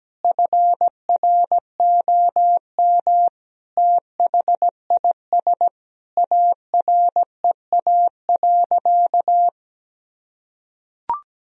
Morse code, heard as a keyed tone, about 17 wpm, 700 Hz. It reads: FROM THIS AREA.
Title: Sentences from Top 1000 Words - Review Part 2 - 17wpm